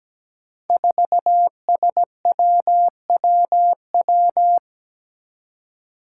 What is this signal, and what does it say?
Morse code, heard as a keyed tone, about 17 wpm, 700 Hz. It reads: 4SWWW
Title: Five Challenging Letter-Number Combinations 17wpm